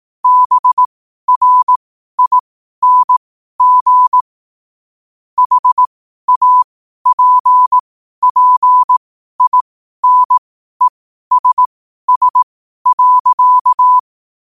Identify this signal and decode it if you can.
Morse code, heard as a keyed tone, about 18 wpm, spaced as for 12 wpm, 1000 Hz. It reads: BRING HAPPINESS.
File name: qotdfiles / 12wpm